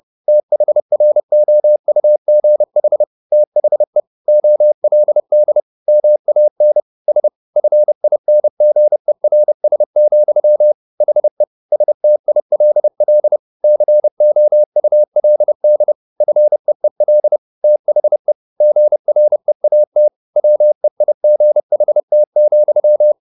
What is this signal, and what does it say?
Morse code, heard as a keyed tone, about 30 wpm, 600 Hz. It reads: THROUGH THE OLD MAN S FINGERS, HE STILL COULD FEEL THE GREAT WEIGHT,
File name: the old man and the sea / the old man and the sea 1 30wpm